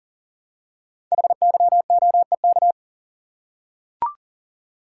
Morse code, heard as a keyed tone, about 39 wpm, 700 Hz. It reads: HYOEK
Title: Code Groups - 5 Letters - 40wpm